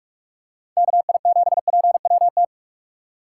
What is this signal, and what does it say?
Morse code, heard as a keyed tone, about 47 wpm, 700 Hz. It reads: KI7PWT